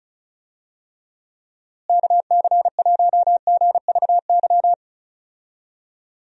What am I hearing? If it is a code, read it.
Morse code, heard as a keyed tone, about 36 wpm, 700 Hz. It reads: KC1GVY
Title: USA Call Signs - 6 Characters - Repeated 2x - 35wpm